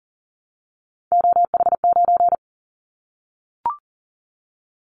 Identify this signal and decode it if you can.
Morse code, heard as a keyed tone, about 40 wpm, 700 Hz. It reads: OH9
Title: Callsign Prefixes for Top 25 DXCC - Repeated 3x - 40wpm